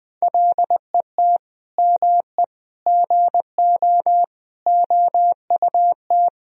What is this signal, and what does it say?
Morse code, heard as a keyed tone, about 20 wpm, 700 Hz. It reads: LET ME GO OUT
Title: Sentences from Top 1000 Words - Review Part 4 - 20wpm